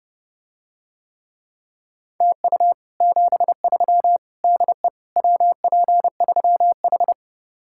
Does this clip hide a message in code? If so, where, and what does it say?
Yes: Morse code from 2.2 to 7.1 s, heard as a keyed tone, about 32 wpm, 700 Hz. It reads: TU 73 DE WP3H